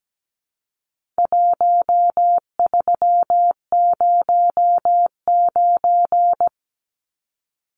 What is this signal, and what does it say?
Morse code, heard as a keyed tone, about 17 wpm, 700 Hz. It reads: 1309
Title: Numbers - 4-Digit 17wpm